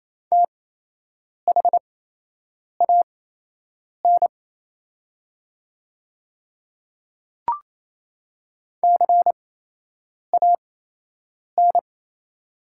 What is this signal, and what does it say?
Morse code, heard as a keyed tone, about 28 wpm, spaced as for 8 wpm, 700 Hz. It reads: THAN CAN
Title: 8x Character Spacing - Top 100 Words - 28wpm